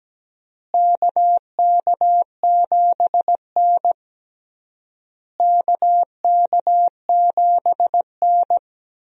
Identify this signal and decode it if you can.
Morse code, heard as a keyed tone, about 17 wpm, 700 Hz. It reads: KK7N KK7N
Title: USA Call Signs - 4 Characters - Repeated 3x - 17wpm